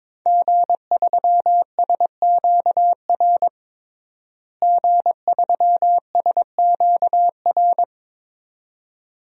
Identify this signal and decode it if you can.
Morse code, heard as a keyed tone, about 22 wpm, 700 Hz. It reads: G3SQR G3SQR
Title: DXCC-100 Call Signs - 5 Characters - Repeated 2x - 22wpm